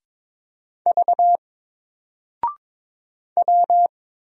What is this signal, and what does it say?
Morse code, heard as a keyed tone, about 22 wpm, 700 Hz. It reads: V W
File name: Single-Letter-Number-22wpm